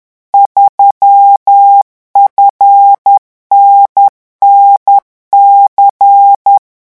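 Morse code, heard as a keyed tone, about 10 wpm, 800 Hz. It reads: 3FNNC
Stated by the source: Morse revision exercise for French radio amateurs at 10 words per minute.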